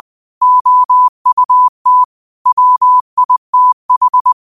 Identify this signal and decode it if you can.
Morse code, heard as a keyed tone, about 20 wpm, 1000 Hz. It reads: OUT WITH